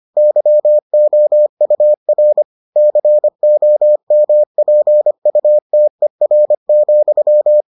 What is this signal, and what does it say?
Morse code, heard as a keyed tone, about 25 wpm, 600 Hz. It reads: YOUR COMPUTER,